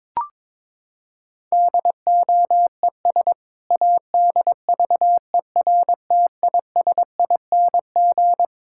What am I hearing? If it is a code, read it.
Morse code, heard as a keyed tone, about 22 wpm, 700 Hz. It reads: DOES ADVERTISING
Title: Sentences from Top 1000 Words - Review Part 6 - 22wpm